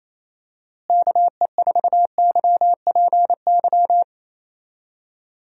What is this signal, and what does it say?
Morse code, heard as a keyed tone, about 28 wpm, 700 Hz. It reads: KE4YPY
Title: USA Call Signs - 6 Characters - Repeated 3x - 28wpm